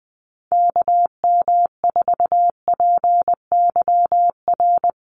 Morse code, heard as a keyed tone, about 20 wpm, 700 Hz. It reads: KM4PYR
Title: USA Call Signs Repeated 2x 20wpm